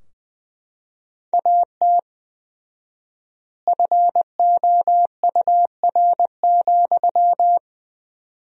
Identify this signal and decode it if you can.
Morse code, heard as a keyed tone, about 21 wpm, 700 Hz. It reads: AT FOUR,
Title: 4x Word-Spacing - Sentences from Top 300 Words - 20wpm